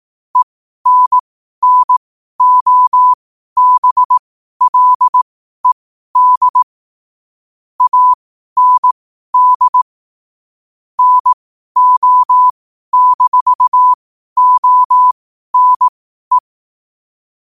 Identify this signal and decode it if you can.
Morse code, heard as a keyed tone, about 18 wpm, spaced as for 12 wpm, 1000 Hz. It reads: ENNOBLED AND NO-ONE
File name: qotdfiles / 12wpm